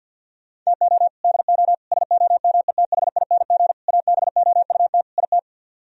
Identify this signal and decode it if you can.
Morse code, heard as a keyed tone, about 64 wpm, 700 Hz. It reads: TO DO SOMETHING ABOUT IT